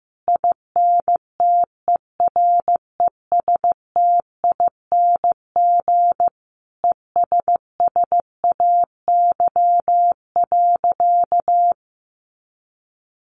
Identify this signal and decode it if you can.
Morse code, heard as a keyed tone, about 15 wpm, 700 Hz. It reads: INTERESTING ESSAY.
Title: Sentences from Top 1500 Words 15wpm